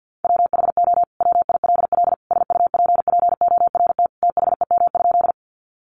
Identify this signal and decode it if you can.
Morse code, heard as a keyed tone, about 50 wpm, 700 Hz. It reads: WHO WILL SUPPORT THEM?
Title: Sentences from Top 800 Words 50wpm